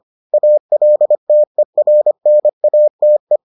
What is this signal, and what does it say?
Morse code, heard as a keyed tone, about 25 wpm, 600 Hz. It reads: ALTERNATE